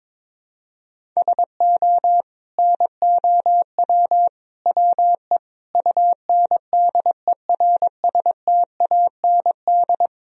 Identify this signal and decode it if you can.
Morse code, heard as a keyed tone, about 22 wpm, 700 Hz. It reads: SO NOW WE UNDERSTAND
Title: Sentences from Top 1000 Words - Review Part 1 - 22wpm